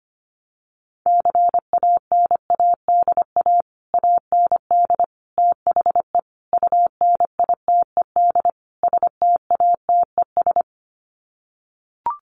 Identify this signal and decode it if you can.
Morse code, heard as a keyed tone, about 25 wpm, 700 Hz. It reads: CANADA AND THE UNITED STATES
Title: Sets of 5 Words Encore 25wpm